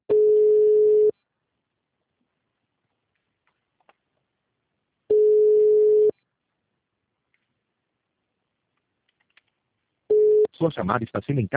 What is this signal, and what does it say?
Ringback tone (Europe, 425 Hz), with speech.
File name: out- -2003- - - .26268